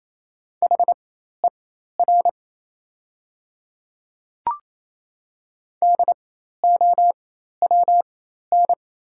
Morse code, heard as a keyed tone, about 28 wpm, spaced as for 13 wpm, 700 Hz. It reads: HER DOWN
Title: 4x Character Spacing - Top 200 Words - 28wpm